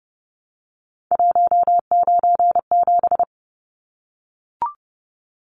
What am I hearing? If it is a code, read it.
Morse code, heard as a keyed tone, about 30 wpm, 700 Hz. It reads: 197
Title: Numbers - 3-Digit 30wpm